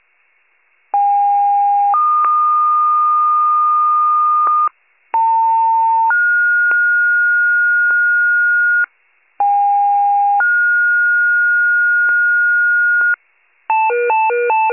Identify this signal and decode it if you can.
Two-tone page, heard as a one-second tone, then a three-second tone, about 810 then 1220 Hz, about 900 then 1430 Hz, about 810 then 1430 Hz, with noise. It is heard from 0.9 to 13.1 s.